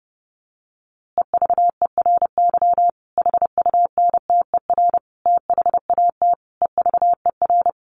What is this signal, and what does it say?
Morse code, heard as a keyed tone, about 30 wpm, 700 Hz. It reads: EVERY HUNTER THAT EVER
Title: A Sound Of Thunder 30wpm